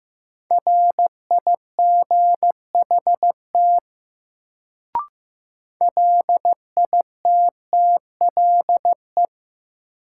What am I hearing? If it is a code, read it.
Morse code, heard as a keyed tone, about 15 wpm, 700 Hz. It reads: RIGHT LITTLE
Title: Sentences from Top 200 Words - New Word Drill - 15wpm